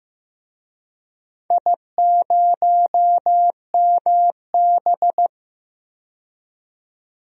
Morse code, heard as a keyed tone, about 15 wpm, 700 Hz. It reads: I0MB
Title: DXCC-100 Call Signs - 4 Characters - 15wpm